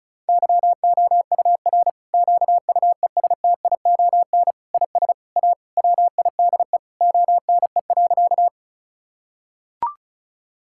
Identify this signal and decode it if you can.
Morse code, heard as a keyed tone, about 36 wpm, 700 Hz. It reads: YOUR QUESTION IS A WIDE ONE.
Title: Sentences from Top 1000 Words - Review Part 6 - 35wpm